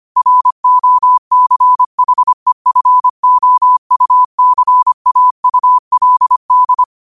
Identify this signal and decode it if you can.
Morse code, heard as a keyed tone, about 25 wpm, 1000 Hz. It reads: ROCHEFOUCAULD
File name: qotdfiles / 25wpm